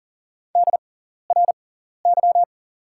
Morse code, heard as a keyed tone, about 40 wpm, spaced as for 14 wpm, 700 Hz. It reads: DRY
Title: 6x Character Spacing - Top 500 Words - 40wpm